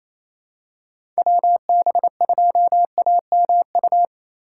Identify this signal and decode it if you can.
Morse code, heard as a keyed tone, about 28 wpm, 700 Hz. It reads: WB2AMU
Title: USA Call Signs - 6 Characters - 4x Character Spacing - 28wpm